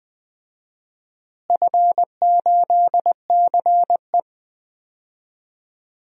Morse code, heard as a keyed tone, about 20 wpm, 700 Hz. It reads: F8CE
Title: DXCC-100 Call Signs - 4 Characters - Repeated 3x - 20wpm